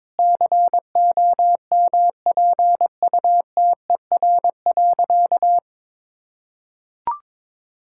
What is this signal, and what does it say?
Morse code, heard as a keyed tone, about 22 wpm, 690 Hz. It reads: COMPUTER.
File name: practice / 22wpm / Easy Sentences with Hints - 22wpm